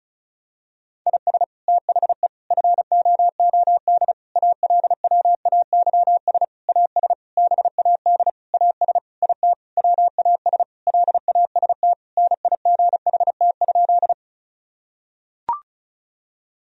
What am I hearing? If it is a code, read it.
Morse code, heard as a keyed tone, about 36 wpm, 700 Hz. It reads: IS THE FOOD ALWAYS AS BAD AS IT WAS LAST NIGHT?
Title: Sentences from Top 1000 Words - Review Part 1 - 35wpm